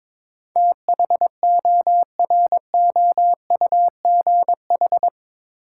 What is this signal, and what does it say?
Morse code, heard as a keyed tone, about 22 wpm, 700 Hz. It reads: THOROUGH
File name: practice / 22wpm / Eight Letter Words - 22wpm